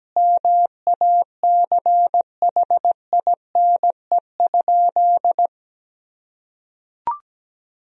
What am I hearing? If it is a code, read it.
Morse code, heard as a keyed tone, about 17 wpm, 700 Hz. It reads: MACHINE?
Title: Sentences from Top 1000 Words - Review Part 2 - 17wpm